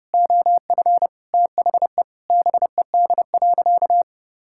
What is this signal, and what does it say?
Morse code, heard as a keyed tone, about 30 wpm, 700 Hz. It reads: OF THE BED.
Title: Sentences from Top 1000 Words - Review Part 6 - 30wpm